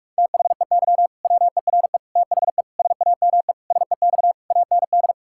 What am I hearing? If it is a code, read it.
Morse code, heard as a keyed tone, about 41 wpm, 700 Hz. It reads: THEY WERE THE SAME SEX AND
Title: Sentences from Top 1600 Words 45wpm